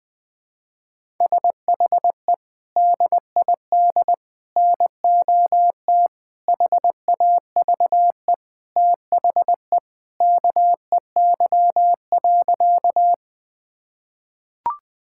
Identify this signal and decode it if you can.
Morse code, heard as a keyed tone, about 21 wpm, 700 Hz. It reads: SHE DID NOT HAVE THE KEY.
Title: Sentences with 4-Word Phrases - 20wpm